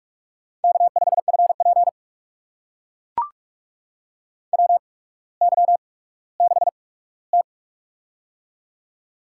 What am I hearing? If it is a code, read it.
Morse code, heard as a keyed tone, about 44 wpm, spaced as for 19 wpm, 700 Hz. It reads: K5FP WY6T
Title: USA Call Signs - 4 Characters - 8x Character Spacing - 45wpm